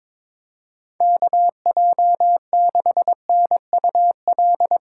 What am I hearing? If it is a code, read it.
Morse code, heard as a keyed tone, about 22 wpm, 700 Hz. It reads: KJ6NUL
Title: USA Call Signs - 6 Characters - 4x Character Spacing - 22wpm